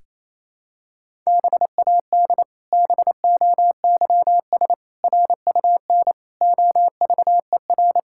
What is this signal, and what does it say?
Morse code, heard as a keyed tone, about 28 wpm, 700 Hz. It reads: BAD BOYS RUN OVER
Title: Pangram Challenge v2 - 28wpm